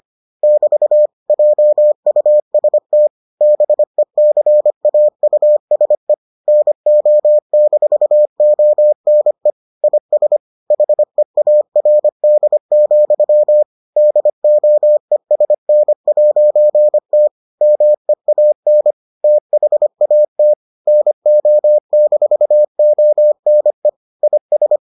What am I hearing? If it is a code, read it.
Morse code, heard as a keyed tone, about 25 wpm, 600 Hz. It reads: <BT> JUST BECAUSE NO-ONE IS HEARD, DOESN'T MEAN THAT NO-ONE IS